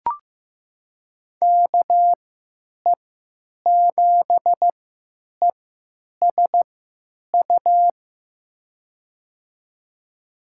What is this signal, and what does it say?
Morse code, heard as a keyed tone, about 15 wpm, spaced as for 9 wpm, 700 Hz. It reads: KE7ESU
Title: USA Call Signs - 6 Characters - 3x Character Spacing - 15wpm